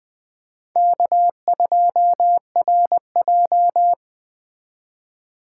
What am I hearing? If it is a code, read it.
Morse code, heard as a keyed tone, about 19 wpm, 700 Hz. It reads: K2RJ